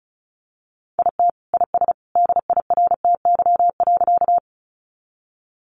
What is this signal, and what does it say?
Morse code, heard as a keyed tone, about 34 wpm, 700 Hz. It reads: IT IS DIRTY.
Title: Additional Easy Phrases 35wpm